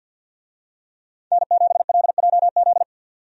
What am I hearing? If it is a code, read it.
Morse code, heard as a keyed tone, about 52 wpm, 700 Hz. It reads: N7LJZ
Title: USA Call Signs - 5 Characters - 50wpm